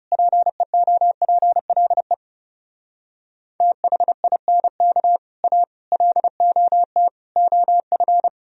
Morse code, heard as a keyed tone, about 40 wpm, 700 Hz. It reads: PEOPLE THINK A LOT OF